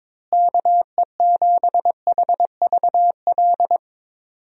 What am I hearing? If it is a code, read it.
Morse code, heard as a keyed tone, about 22 wpm, 700 Hz. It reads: KE7HVL
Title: USA Call Signs - 6 Characters - Repeated 2x - 22wpm